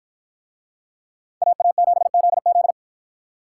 Morse code, heard as a keyed tone, about 53 wpm, 700 Hz. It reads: AA7ZZ